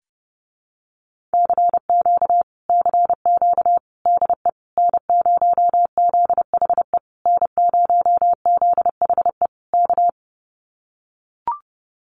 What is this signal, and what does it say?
Morse code, heard as a keyed tone, about 30 wpm, 700 Hz. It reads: CQ CQ DE N0ZHE N0ZHE K
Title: Common QSO Phrases Encore 30wpm